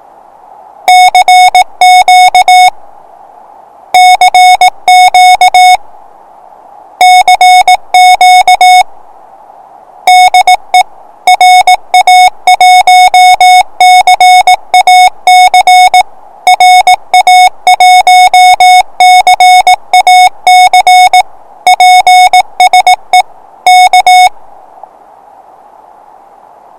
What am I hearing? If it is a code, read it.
Morse code, heard as a keyed tone, about 18 wpm, 730 Hz. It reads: CQ CQ CQ DE RA1CAC RA1CAC PSE K
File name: attachments / 1cq noagc 2cq agcin 3cq agcout